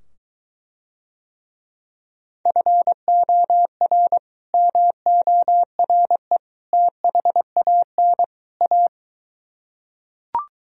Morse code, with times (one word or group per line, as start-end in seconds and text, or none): 2.5-4.2 FOR
4.5-6.4 MORE
6.7-8.2 THAN
8.6-8.9 A